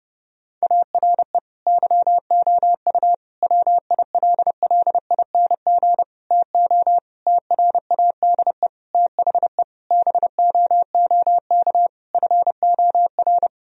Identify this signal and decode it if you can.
Morse code, heard as a keyed tone, about 30 wpm, 700 Hz. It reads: ARE YOU WILLING TO TRADE THE BOOK FOR
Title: Sentences from Top 1000 Words - Review Part 4 - 30wpm